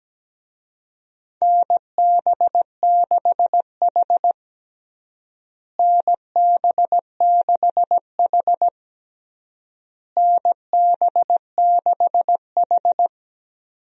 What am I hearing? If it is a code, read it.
Morse code, heard as a keyed tone, about 17 wpm, 700 Hz. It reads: NB6H NB6H NB6H
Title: USA Call Signs - 4 Characters - Repeated 3x - 17wpm